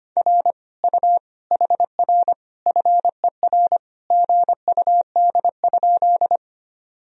Morse code, heard as a keyed tone, about 25 wpm, 700 Hz. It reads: R U HR FER GUD?
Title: Sentences from Top 200 Words and Abbreviations 25wpm